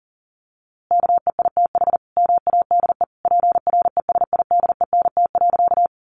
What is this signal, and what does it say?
Morse code, heard as a keyed tone, about 40 wpm, 700 Hz. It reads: KEITH MADE PRESIDENT.
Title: A Sound Of Thunder 40wpm